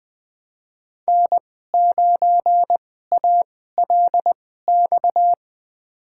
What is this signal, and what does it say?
Morse code, heard as a keyed tone, about 20 wpm, spaced as for 14 wpm, 700 Hz. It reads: N9ALX
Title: USA Call Signs - 5 Characters - 2x Character Spacing - 20wpm